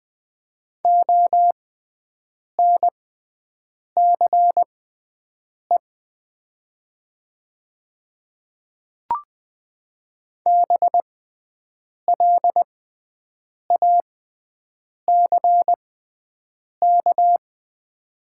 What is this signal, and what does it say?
Morse code, heard as a keyed tone, about 20 wpm, spaced as for 7 wpm, 700 Hz. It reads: ONCE BLACK
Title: 6x Character Spacing - Top 500 Words - 20wpm